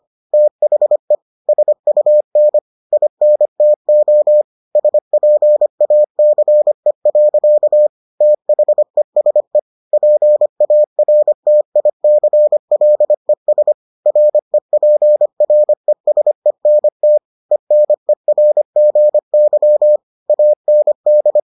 Morse code, heard as a keyed tone, about 25 wpm, 600 Hz. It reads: THE SUN INTO SPACE. THESE PARTICLES REPRESENT ENERGY AND